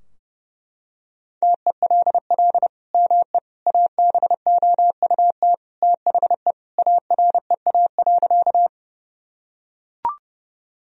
Morse code, heard as a keyed tone, about 32 wpm, 700 Hz. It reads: TELL ME ABOUT THE AREA.
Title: Sentences from Top 1000 Words - Review Part 2 - 30wpm